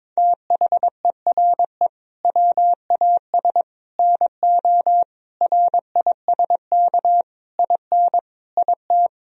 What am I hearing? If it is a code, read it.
Morse code, heard as a keyed tone, about 22 wpm, 700 Hz. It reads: THERE WAS NO RISK IN IT
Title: Sentences from Top 1000 Words - Review Part 4 - 22wpm